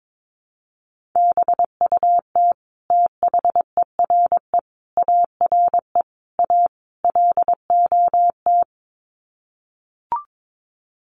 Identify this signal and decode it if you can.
Morse code, heard as a keyed tone, about 22 wpm, 700 Hz. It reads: BUT THERE ARE A LOT
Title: Sets of 5 Words Encore2 22wpm